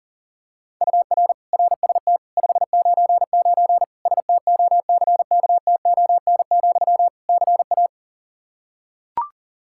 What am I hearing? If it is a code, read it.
Morse code, heard as a keyed tone, about 40 wpm, 700 Hz. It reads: UR RST 599 STOCKTON, CA
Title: K1USN SST QSO Practice 40wpm